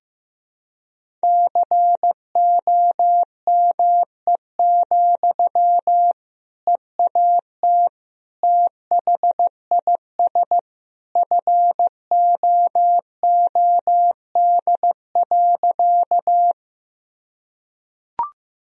Morse code, heard as a keyed tone, about 16 wpm, 700 Hz. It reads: COME, EAT THIS FOOD.
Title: Sentences from Top 1000 Words - Review Part 1 - 15wpm